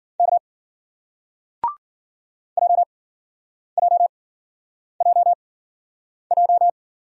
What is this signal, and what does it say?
Morse code, heard as a keyed tone, about 54 wpm, spaced as for 9 wpm, 700 Hz. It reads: K JJJJ